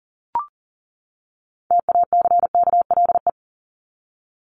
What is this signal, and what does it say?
Morse code, heard as a keyed tone, about 40 wpm, 700 Hz. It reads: TACKLE
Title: Top 1751-2000 Words 40wpm